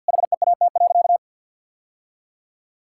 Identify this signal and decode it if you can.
Morse code, heard as a keyed tone, about 50 wpm, 700 Hz. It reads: HEAT.
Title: Sentences from Top 1000 Words - Review Part 6 - 50wpm